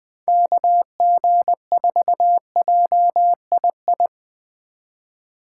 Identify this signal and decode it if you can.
Morse code, heard as a keyed tone, about 20 wpm, 700 Hz. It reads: KG4JII
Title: USA Call Signs - 6 Characters - 4x Character Spacing - 20wpm